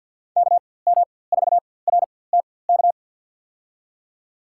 Morse code, heard as a keyed tone, about 48 wpm, spaced as for 23 wpm, 700 Hz. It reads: KM4RTK
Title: USA Call Signs - 6 Characters - 4x Character Spacing - 50wpm